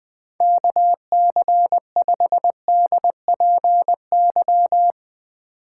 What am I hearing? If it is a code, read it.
Morse code, heard as a keyed tone, about 20 wpm, 700 Hz. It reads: KC5DPY
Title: Speed Racing - USA 2x3 Call Signs 17wpm